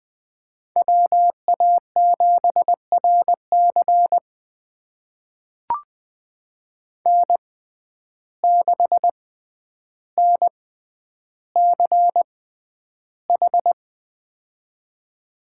Morse code, heard as a keyed tone, about 21 wpm, spaced as for 10 wpm, 700 Hz. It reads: WA7RC N6NCH
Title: USA Call Signs - 5 Characters - 6x Character Spacing - 20wpm